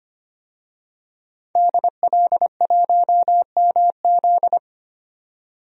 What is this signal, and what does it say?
Morse code, heard as a keyed tone, about 25 wpm, 700 Hz. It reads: DL1MZ
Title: DXCC-100 Call Signs - 5 Characters - 25wpm